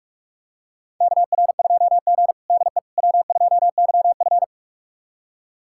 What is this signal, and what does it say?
Morse code, heard as a keyed tone, about 46 wpm, 700 Hz. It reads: KR2G DE W2YF
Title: USA Callsign DE Callsign - 4 Characters - 45wpm